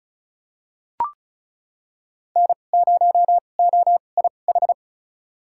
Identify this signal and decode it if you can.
Morse code, heard as a keyed tone, about 34 wpm, spaced as for 25 wpm, 700 Hz. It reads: N0OIH
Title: USA Call Signs - 5 Characters - 2x Character Spacing - 35wpm